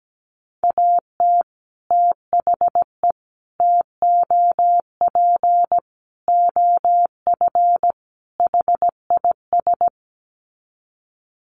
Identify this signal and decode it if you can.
Morse code, heard as a keyed tone, about 17 wpm, 700 Hz. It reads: AT THE TOP OF HIS
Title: Sets of 5 Words Encore 17wpm